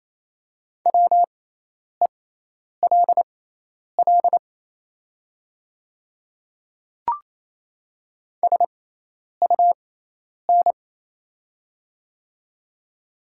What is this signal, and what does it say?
Morse code, heard as a keyed tone, about 27 wpm, spaced as for 10 wpm, 690 Hz. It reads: WELL SUN
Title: 6x Character Spacing - Top 500 Words - 28wpm